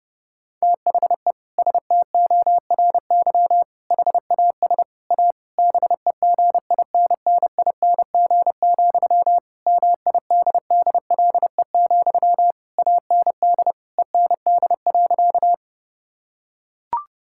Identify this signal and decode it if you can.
Morse code, heard as a keyed tone, about 30 wpm, 700 Hz. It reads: THE STORY HAS A BEGINNING, MIDDLE, AND END.
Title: Sentences from Top 1000 Words - Review Part 6 - 30wpm